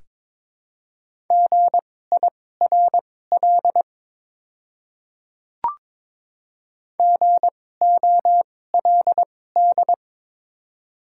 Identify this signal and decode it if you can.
Morse code, heard as a keyed tone, about 21 wpm, spaced as for 16 wpm, 700 Hz. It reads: GIRL GOLD
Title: 2x Character Spacing - Top 500 Words - 22wpm